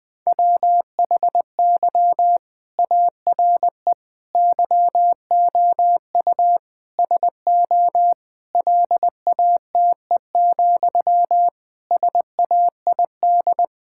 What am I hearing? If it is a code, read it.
Morse code, heard as a keyed tone, about 20 wpm, 700 Hz. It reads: WHY ARE YOU SO LATE, SAID
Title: Sentences from Top 1000 Words - Review Part 6 - 20wpm